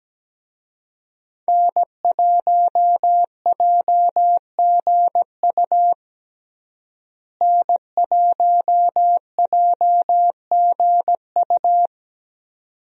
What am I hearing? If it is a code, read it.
Morse code, heard as a keyed tone, about 17 wpm, 700 Hz. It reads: N1JGU N1JGU
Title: USA Call Signs - 5 Characters - Repeated 2x - 17wpm